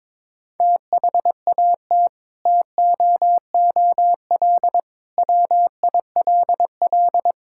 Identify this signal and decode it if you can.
Morse code, heard as a keyed tone, about 22 wpm, 700 Hz. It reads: THAT TOOL WILL
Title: Sentences from Top 1000 Words - Review Part 6 - 22wpm